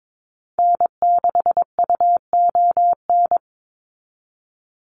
Morse code, heard as a keyed tone, about 22 wpm, 700 Hz. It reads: N6UON
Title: USA Call Signs 22wpm